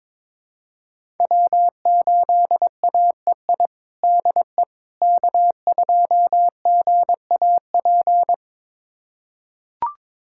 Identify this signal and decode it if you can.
Morse code, heard as a keyed tone, about 22 wpm, 700 Hz. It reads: W8AEI DE K2GAP
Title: USA Callsign DE Callsign - 5 Characters - 22wpm